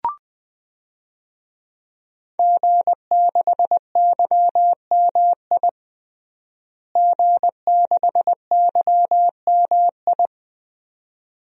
Morse code, heard as a keyed tone, about 21 wpm, 700 Hz. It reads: G6YMI G6YMI
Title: DXCC-100 Call Signs - 5 Characters - Repeated 2x - 20wpm